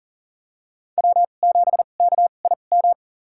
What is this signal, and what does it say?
Morse code, heard as a keyed tone, about 40 wpm, spaced as for 29 wpm, 700 Hz. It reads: W7KIM
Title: USA Call Signs - 2x Character Spacing - 40wpm